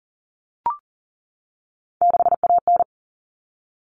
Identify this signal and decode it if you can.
Morse code, heard as a keyed tone, about 40 wpm, 700 Hz. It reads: 6AN
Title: Common Cut Numbers - 3-Digit 40wpm